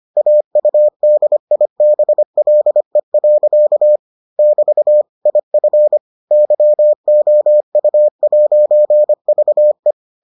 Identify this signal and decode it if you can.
Morse code, heard as a keyed tone, about 25 wpm, 600 Hz. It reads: AUDIBLE. <BT> IF YOU'VE